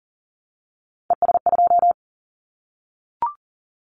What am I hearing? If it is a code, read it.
Morse code, heard as a keyed tone, about 42 wpm, 700 Hz. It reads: ES2
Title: Callsign Prefixes for Top 50 DXCC - 40wpm